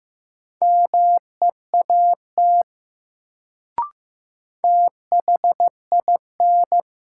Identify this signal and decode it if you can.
Morse code, heard as a keyed tone, about 15 wpm, 700 Hz. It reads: MEAT THIN
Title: Sentences from Top 1200 Words - New Word Drill Alt - 15wpm